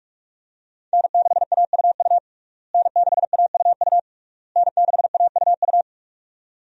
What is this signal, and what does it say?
Morse code, heard as a keyed tone, about 42 wpm, 700 Hz. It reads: N6AUU N6AUU N6AUU
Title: USA Call Signs - 5 Characters - Repeated 3x - 45wpm